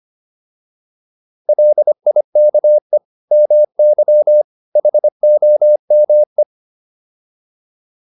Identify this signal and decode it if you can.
Morse code, heard as a keyed tone, about 25 wpm, 600 Hz. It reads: LIKE MY HOME